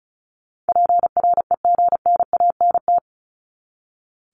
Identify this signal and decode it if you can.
Morse code, heard as a keyed tone, about 34 wpm, 700 Hz. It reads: PREGNANT
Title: Top 751-1000 Words 35wpm